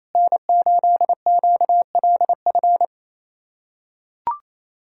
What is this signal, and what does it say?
Morse code, heard as a keyed tone, about 28 wpm, 700 Hz. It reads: N8QLF
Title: USA Callsign DE Callsign - 5 Characters - 28wpm